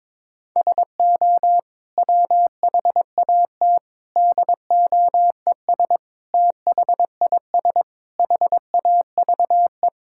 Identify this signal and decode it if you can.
Morse code, heard as a keyed tone, about 22 wpm, 700 Hz. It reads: SO WHAT DOES THIS HAVE
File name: practice / 22wpm / Sets of 8 Words - 22wpm